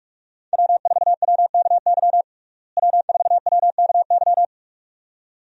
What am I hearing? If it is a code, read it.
Morse code, heard as a keyed tone, about 43 wpm, 700 Hz. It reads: W4WKY W4WKY
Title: USA Call Signs - 5 Characters - Repeated 2x - 45wpm